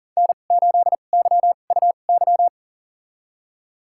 Morse code, heard as a keyed tone, about 40 wpm, spaced as for 29 wpm, 700 Hz. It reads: N8YUY
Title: USA Call Signs - 5 Characters - 2x Character Spacing - 40wpm